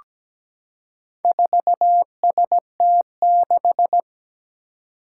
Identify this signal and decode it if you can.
Morse code, heard as a keyed tone, about 17 wpm, 700 Hz. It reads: 4ST6
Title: Four Challenging Letter-Number Combinations 17wpm